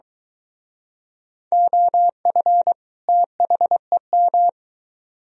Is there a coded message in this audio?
Yes: Morse code from 1.5 to 4.5 s, heard as a keyed tone, about 23 wpm, 700 Hz. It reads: OF THEM